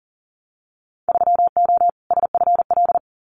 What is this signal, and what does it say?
Morse code, heard as a keyed tone, about 41 wpm, 700 Hz. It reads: 3O SFL
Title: Winter Field Day Exchanges 40wpm